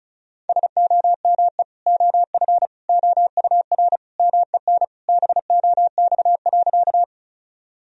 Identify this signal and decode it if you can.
Morse code, heard as a keyed tone, about 34 wpm, 700 Hz. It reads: SOME OF OUR MEN BOX.
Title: Sentences from Top 1000 Words - Review Part 2 - 35wpm